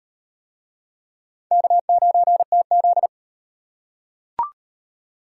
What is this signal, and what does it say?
Morse code, heard as a keyed tone, about 35 wpm, 700 Hz. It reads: K9TZ